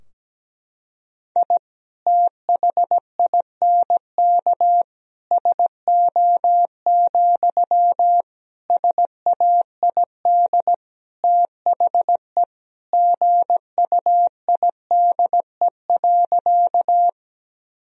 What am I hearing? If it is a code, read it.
Morse code, heard as a keyed tone, about 17 wpm, 700 Hz. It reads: I THINK SO, SAID THE GUIDE.
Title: Sentences from Top 1000 Words - Review Part 3 - 17wpm